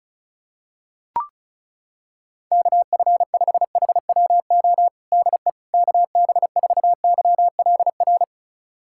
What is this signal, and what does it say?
Morse code, heard as a keyed tone, about 34 wpm, 700 Hz. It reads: KF5HWO DE KB4YLR
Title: USA Callsign DE Callsign - 6 Characters - 35wpm